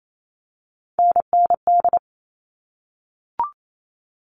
Morse code, heard as a keyed tone, about 28 wpm, 700 Hz. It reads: NND
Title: All Cut Numbers - 3-Digit 28wpm